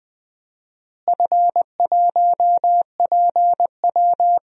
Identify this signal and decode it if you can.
Morse code, heard as a keyed tone, about 20 wpm, 700 Hz. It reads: F1PW
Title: DXCC-100 Call Signs - 4 Characters - Repeated 2x - 20wpm